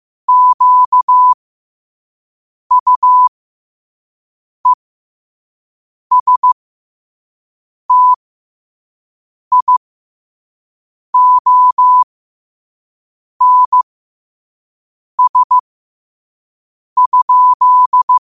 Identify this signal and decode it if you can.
Morse code, heard as a keyed tone, about 15 wpm, spaced as for 5 wpm, 1000 Hz. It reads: QUESTIONS?
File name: qotdfiles / 5wpm